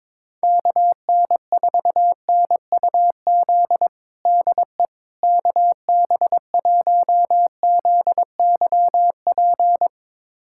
Morse code, heard as a keyed tone, about 22 wpm, 700 Hz. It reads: KN4NUZ DE KB1ZYP
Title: USA Callsign DE Callsign 22wpm